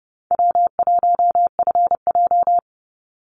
Morse code, heard as a keyed tone, about 30 wpm, 700 Hz. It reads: W1FJ
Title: Easy Call Signs Repeated 3x - 30wpm